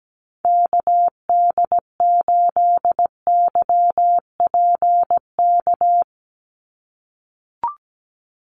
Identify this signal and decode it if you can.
Morse code, heard as a keyed tone, about 17 wpm, 700 Hz. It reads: KD8YPK
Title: USA Call Signs 17wpm